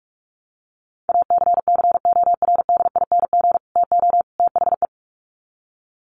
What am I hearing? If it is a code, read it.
Morse code, heard as a keyed tone, about 44 wpm, 700 Hz. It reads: ACCORDING TO THE
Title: Sets of 3 Words 45wpm